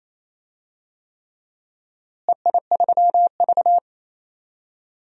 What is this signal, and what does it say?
Morse code, heard as a keyed tone, about 28 wpm, 700 Hz. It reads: EI3V
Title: DXCC-100 Call Signs - 4 Characters - Repeated 3x - 28wpm